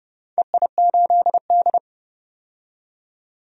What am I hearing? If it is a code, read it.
Morse code, heard as a keyed tone, about 30 wpm, 700 Hz. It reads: EI8D
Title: DXCC-100 Call Signs - 4 Characters - 30wpm